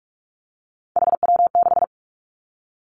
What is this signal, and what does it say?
Morse code, heard as a keyed tone, about 46 wpm, 700 Hz. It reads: HW6
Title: Three Challenging Letter-Number Combinations 45wpm